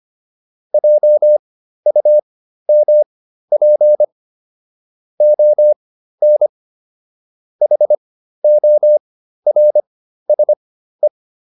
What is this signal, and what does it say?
Morse code, heard as a keyed tone, about 25 wpm, spaced as for 13 wpm, 600 Hz. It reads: JUMP ON HORSE